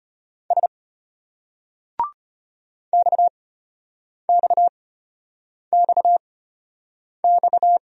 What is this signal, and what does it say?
Morse code, heard as a keyed tone, about 33 wpm, spaced as for 8 wpm, 700 Hz. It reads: S XXXX